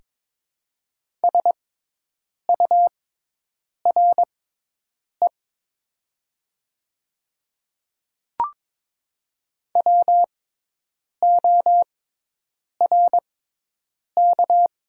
Morse code, heard as a keyed tone, about 22 wpm, spaced as for 8 wpm, 700 Hz. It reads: SURE WORK
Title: 6x Character Spacing - Top 300 Words - 22wpm